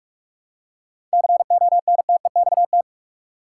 Morse code, heard as a keyed tone, about 46 wpm, 700 Hz. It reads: CONTEXT
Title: Seven Letter Words - 45wpm